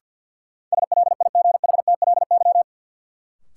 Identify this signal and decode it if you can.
Morse code, heard as a keyed tone, about 50 wpm, 700 Hz. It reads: SLIGHTLY
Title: Sentences from Top 1300 Words - New Word Drill - 50wpm